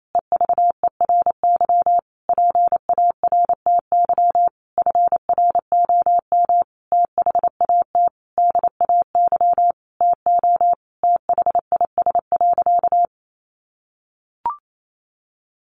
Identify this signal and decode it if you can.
Morse code, heard as a keyed tone, about 28 wpm, 700 Hz. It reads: EVERY PARTY FROM THAT DAY TO THIS.
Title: Sentences from Top 1400 Words 28wpm